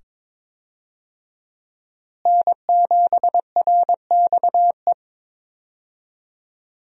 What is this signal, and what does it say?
Morse code, heard as a keyed tone, about 21 wpm, 700 Hz. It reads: N7RXE
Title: USA Call Signs - 5 Characters - Repeated 2x - 22wpm